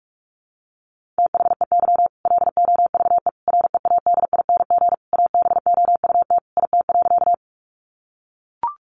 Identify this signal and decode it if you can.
Morse code, heard as a keyed tone, about 44 wpm, 700 Hz. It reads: THEY LOVE READING ABOUT IT.
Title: Sentences from Top 600 Words 45wpm